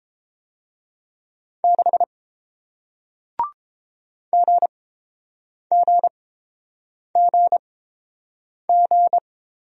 Morse code, heard as a keyed tone, about 30 wpm, spaced as for 7 wpm, 700 Hz. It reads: 6 GGGG